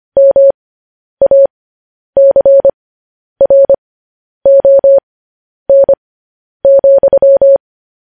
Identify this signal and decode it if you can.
Morse code, heard as a keyed tone, about 25 wpm, spaced as for 10 wpm, 550 Hz. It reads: MACRON,
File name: podcast / News headlines-1x-25x10